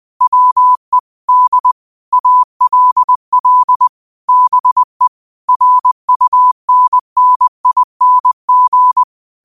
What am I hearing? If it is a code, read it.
Morse code, heard as a keyed tone, about 20 wpm, 1000 Hz. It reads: WE D ALL BE RUNNING
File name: qotdfiles / 20wpm